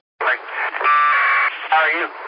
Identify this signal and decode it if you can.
Packet radio (AX.25, as in APRS), with speech.